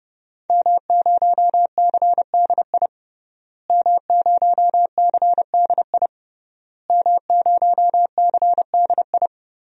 Morse code, heard as a keyed tone, about 32 wpm, 700 Hz. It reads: M0CDI M0CDI M0CDI
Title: DXCC-100 Call Signs - 5 Characters - Repeated 3x - 30wpm